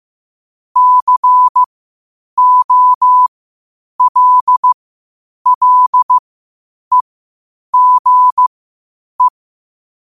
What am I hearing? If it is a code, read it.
Morse code, heard as a keyed tone, about 15 wpm, spaced as for 8 wpm, 1000 Hz. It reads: COLLEGE
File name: qotdfiles / 8wpm